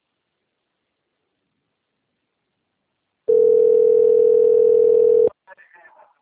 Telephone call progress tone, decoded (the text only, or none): ringback tone